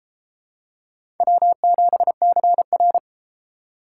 Morse code, heard as a keyed tone, about 32 wpm, 700 Hz. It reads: W7CR